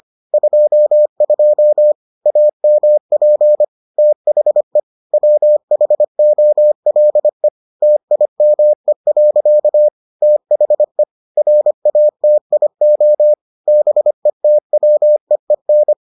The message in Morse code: 22 AMP THE WHOLE TIME. THE RATIO BETWEEN

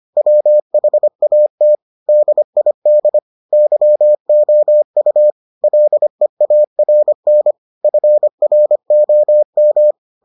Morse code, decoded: WHAT DID YOU LEARN FROM

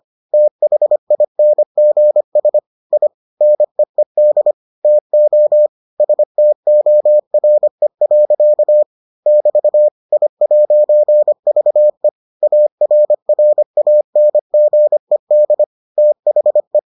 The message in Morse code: THINGS I NEED TO STORE. <BT> I'VE ARRANGED THE

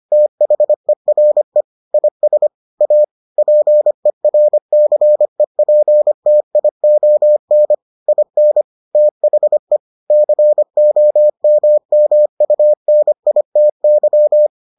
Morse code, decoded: THERE IS A PERCEPTION IN THE COMMUNITY

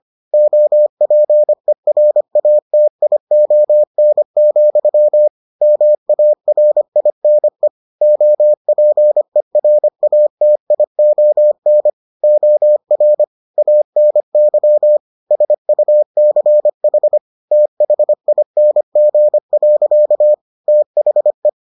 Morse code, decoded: OPERATION, MARINE OPERATION OR ANY SUCH THING. THE

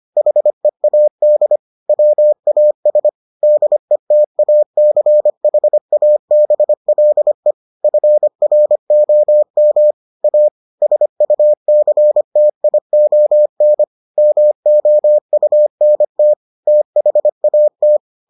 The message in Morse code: HEAD WAS DETACHABLE FROM A SUCTION MOUNT THAT

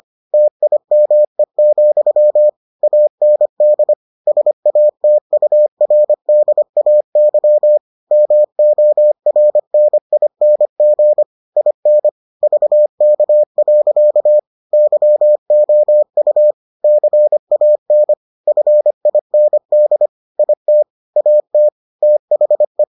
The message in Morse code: TIME, AND SATURDAY MORNING IN VK. YOU CAN FIND IT AT THE

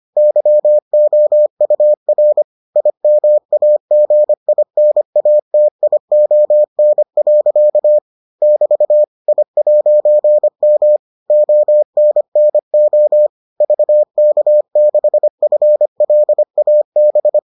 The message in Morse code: YOUR IMAGINATION. <BT> I'M ONNO VK6FLAB

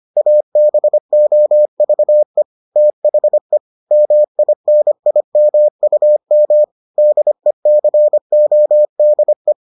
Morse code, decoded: ABOVE THE MINIMUM DECODE

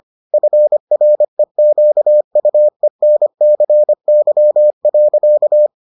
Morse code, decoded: FREQUENCY.

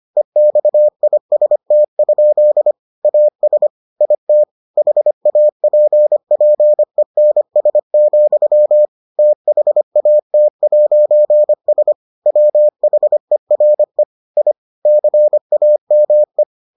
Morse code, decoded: EXIST? AS IT HAPPENS, THAT'S WHERE I CAME

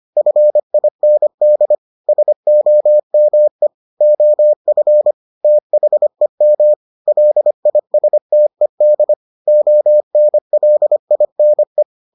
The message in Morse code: FIND SOME OF THEM LISTED ONLINE

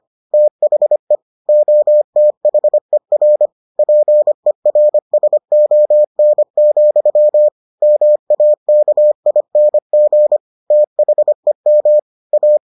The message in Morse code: THE OTHER PERSON, MAKING THEM A

